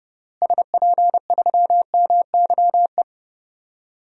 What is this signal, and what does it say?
Morse code, heard as a keyed tone, about 30 wpm, 700 Hz. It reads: SP3MYE
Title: DXCC-100 Call Signs - 6 Characters - Repeated 3x - 30wpm